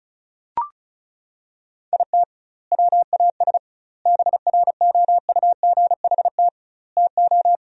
Morse code, heard as a keyed tone, about 34 wpm, 700 Hz. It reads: IT WAS BROUGHT TO
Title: 2x Word-Spacing - Sentences from Top 500 Words - 35wpm